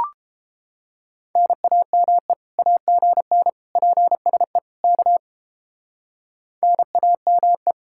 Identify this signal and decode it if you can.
Morse code, heard as a keyed tone, about 31 wpm, 700 Hz. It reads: NAME AGN PSE K NAME